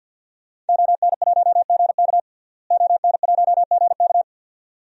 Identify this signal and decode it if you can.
Morse code, heard as a keyed tone, about 47 wpm, 700 Hz. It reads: ON1GK ON1GK
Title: DXCC-100 Call Signs - 5 Characters - Repeated 2x - 50wpm